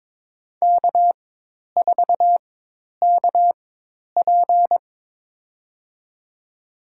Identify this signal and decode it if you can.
Morse code, heard as a keyed tone, about 22 wpm, spaced as for 10 wpm, 700 Hz. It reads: K4KP
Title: USA Call Signs - 4 Characters - 4x Character Spacing - 22wpm